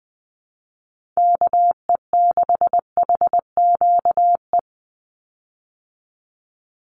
Morse code, heard as a keyed tone, about 20 wpm, 700 Hz. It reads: KE6HQE
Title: USA Call Signs 20wpm